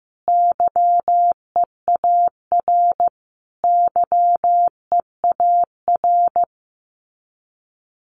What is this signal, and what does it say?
Morse code, heard as a keyed tone, about 15 wpm, 700 Hz. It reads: YEAR YEAR
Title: Top 500 Words Repeated 15wpm